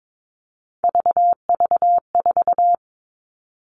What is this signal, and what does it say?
Morse code, heard as a keyed tone, about 22 wpm, 700 Hz. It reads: VV4
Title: All Cut Numbers - 3-Digit 22wpm